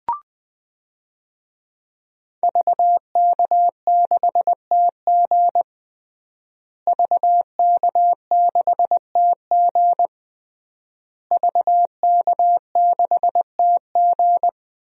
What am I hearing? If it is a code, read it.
Morse code, heard as a keyed tone, about 20 wpm, 700 Hz. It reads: VK6TG VK6TG VK6TG
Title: DXCC-100 Call Signs - 5 Characters - Repeated 3x - 20wpm